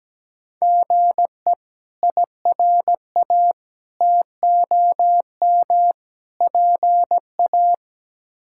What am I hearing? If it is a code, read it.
Morse code, heard as a keyed tone, about 17 wpm, 700 Hz. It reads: GE IRA TOM PA
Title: K1USN SST QSO Practice - 17wpm